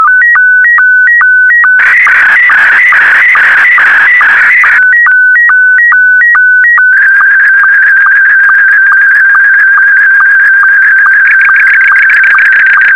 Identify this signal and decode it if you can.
SSTV (slow-scan television), Scottie 1.